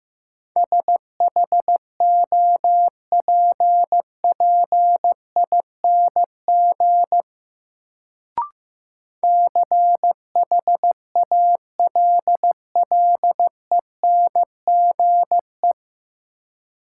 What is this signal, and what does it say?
Morse code, heard as a keyed tone, about 15 wpm, 700 Hz. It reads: SHOPPING CHALLENGE
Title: Sentences from Top 1000 Words - Review Part 7 - New Word Drill - 15wpm